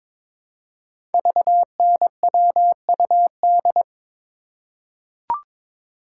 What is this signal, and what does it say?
Morse code, heard as a keyed tone, about 22 wpm, 700 Hz. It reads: VNWUD
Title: Code Groups - 5 Letters - 22wpm